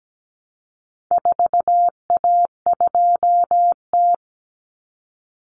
Morse code, heard as a keyed tone, about 17 wpm, 700 Hz. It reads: 4A2T
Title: Common Cut Numbers - 4-Digit 17wpm